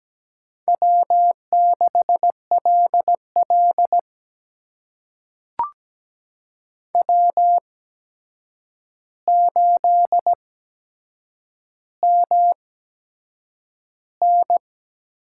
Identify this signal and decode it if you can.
Morse code, heard as a keyed tone, about 17 wpm, spaced as for 7 wpm, 700 Hz. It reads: W6LL W8MN
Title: USA Call Signs - 4 Characters - 8x Character Spacing - 17wpm